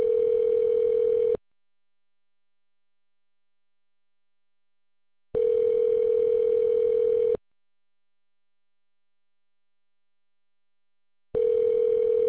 Ringback tone (North America), 440 + 480 Hz.